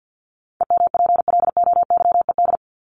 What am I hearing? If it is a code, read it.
Morse code, heard as a keyed tone, about 50 wpm, 700 Hz. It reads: EMPLOYED
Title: ED Suffix - Top 100 Words with Base Words 50wpm